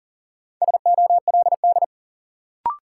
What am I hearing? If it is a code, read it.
Morse code, heard as a keyed tone, about 38 wpm, 700 Hz. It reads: SOLD
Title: Sentences from Top 1400 Words - New Word Drill Alt - 40wpm